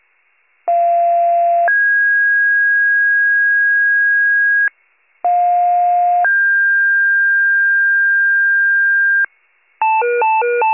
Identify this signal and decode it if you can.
Two-tone page, heard as a one-second tone, then a three-second tone, about 690 then 1650 Hz, about 710 then 1600 Hz. It is heard from 0.7 to 9.3 s.